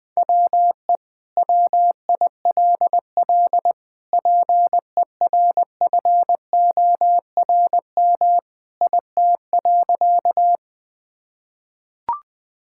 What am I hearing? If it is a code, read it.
Morse code, heard as a keyed tone, about 20 wpm, 700 Hz. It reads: WE WILL PERFORM IT.
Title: Sentences from Top 1000 Words - Review Part 5 - 20wpm